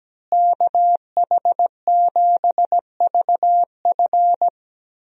Morse code, heard as a keyed tone, about 17 wpm, 700 Hz. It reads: KH7VF
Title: USA Call Signs - 5 Characters - Repeated 2x - 17wpm